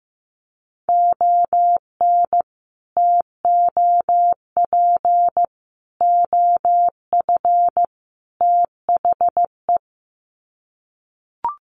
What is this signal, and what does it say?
Morse code, heard as a keyed tone, about 15 wpm, 700 Hz. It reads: ON TOP OF THE
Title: Sets of 4 Words 15wpm